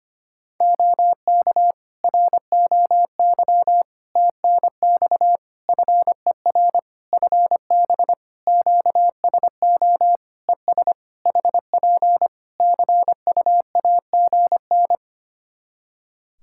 Morse code, heard as a keyed tone, about 25 wpm, 700 Hz. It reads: OK ROY TNX FER FB QSO ES HP CUAGN
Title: Practice QSO - Exchange 4 - Ending - 25wpm